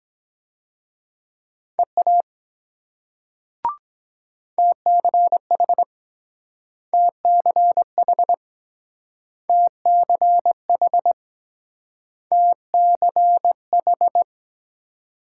Morse code, heard as a keyed tone, about 23 wpm, 700 Hz. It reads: EA TCH TCH TCH TCH